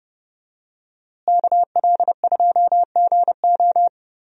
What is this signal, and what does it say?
Morse code, heard as a keyed tone, about 30 wpm, 700 Hz. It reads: KL2GO
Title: USA Call Signs - 5 Characters - Repeated 2x - 30wpm